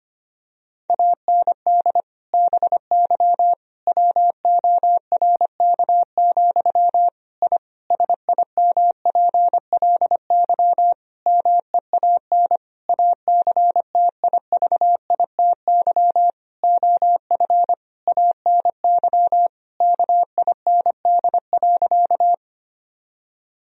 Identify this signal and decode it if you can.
Morse code, heard as a keyed tone, about 25 wpm, 700 Hz. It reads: AND BY WORK, I SIMPLY MEAN ACTIVITY OF ANY KIND.
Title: Sentences from Top 1000 Words - Review Part 4 - 25wpm